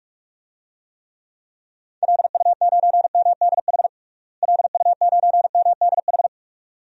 Morse code, heard as a keyed tone, about 43 wpm, 700 Hz. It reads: LU9MDH LU9MDH
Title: DXCC-100 Call Signs - 6 Characters - Repeated 2x - 45wpm